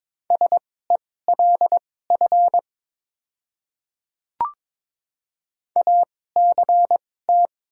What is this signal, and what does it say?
Morse code, heard as a keyed tone, about 21 wpm, spaced as for 16 wpm, 700 Hz. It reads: SELF ACT
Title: 2x Character Spacing - Top 200 Words - 22wpm